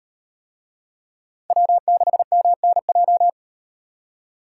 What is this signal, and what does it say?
Morse code, heard as a keyed tone, about 37 wpm, 700 Hz. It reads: W6MNJ